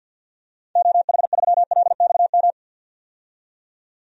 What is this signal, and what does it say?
Morse code, heard as a keyed tone, about 52 wpm, 700 Hz. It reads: OH3LXM
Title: DXCC-100 Call Signs - 6 Characters - 50wpm